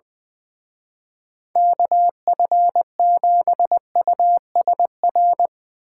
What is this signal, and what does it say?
Morse code, heard as a keyed tone, about 22 wpm, 700 Hz. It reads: KF7USR
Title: Speed Racing - USA 2x3 Call Signs 20wpm